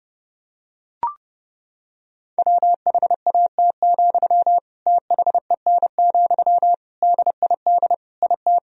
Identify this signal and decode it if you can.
Morse code, heard as a keyed tone, about 32 wpm, 700 Hz. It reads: WHAT, THEN, DID IT
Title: Sentences from Top 1000 Words - Review Part 4 - 30wpm